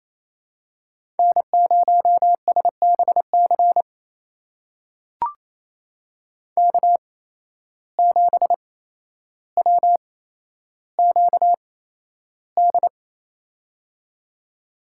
Morse code, heard as a keyed tone, about 28 wpm, spaced as for 12 wpm, 700 Hz. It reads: N0SBC K7WQD
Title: USA Call Signs - 8x Character Spacing - 28wpm